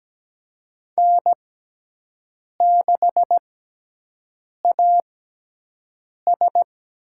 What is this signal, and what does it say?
Morse code, heard as a keyed tone, about 17 wpm, spaced as for 6 wpm, 700 Hz. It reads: N6AS
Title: USA Call Signs - 4 Characters - 6x Character Spacing - 17wpm